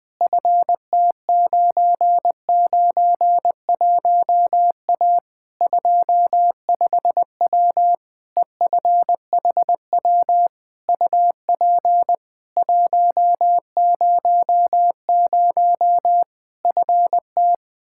Morse code, heard as a keyed tone, about 20 wpm, 700 Hz. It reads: FT991A 25W EFHW UP 100 FT
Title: Practice QSO - Exchange 2 - Info - 20wpm